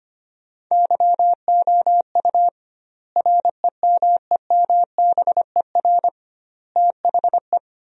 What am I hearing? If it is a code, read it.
Morse code, heard as a keyed tone, about 25 wpm, 700 Hz. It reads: YOU REMEMBER THE
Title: 2x Word-Spacing - Sentences from Top 400 Words - 25wpm